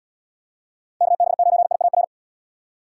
Morse code, heard as a keyed tone, about 71 wpm, 700 Hz. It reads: KB9IAR